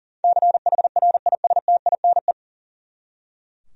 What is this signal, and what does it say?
Morse code, heard as a keyed tone, about 40 wpm, 700 Hz. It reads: CHRISTINE
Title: USA Common Names - 40wpm